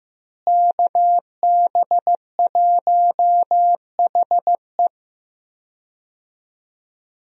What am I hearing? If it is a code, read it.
Morse code, heard as a keyed tone, about 15 wpm, 700 Hz. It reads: KB1HE
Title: USA Call Signs - 5 Characters - Repeated 3x - 15wpm